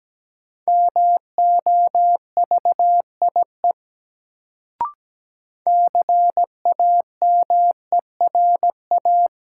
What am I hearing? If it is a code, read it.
Morse code, heard as a keyed tone, about 17 wpm, 700 Hz. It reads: MOVIE CAMERA
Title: Sentences from Top 1000 Words - Review Part 5 - New Word Drill - 17wpm